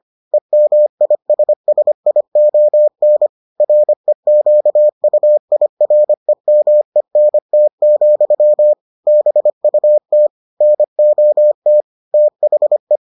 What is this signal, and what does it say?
Morse code, heard as a keyed tone, about 25 wpm, 600 Hz. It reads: EMISSION REQUIREMENT, BUT NOT THE